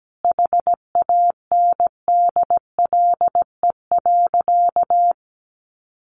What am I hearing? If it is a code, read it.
Morse code, heard as a keyed tone, about 17 wpm, 700 Hz. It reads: HANDLE.
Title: Sentences from Top 1100 Words 17wpm